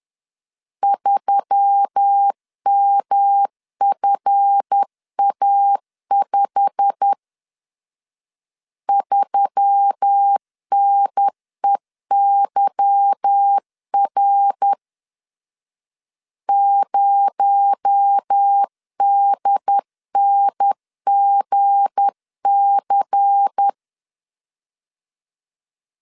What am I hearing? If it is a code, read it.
Morse code, heard as a keyed tone, about 10 wpm, 800 Hz. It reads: 3MFA5 3NEYR 0DNGC
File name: LEZIONI / 8 YJ45 / 8 YJ45 10wpm 2549